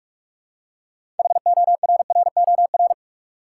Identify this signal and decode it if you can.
Morse code, heard as a keyed tone, about 46 wpm, 700 Hz. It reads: HORROR
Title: Six Letter Words - 45wpm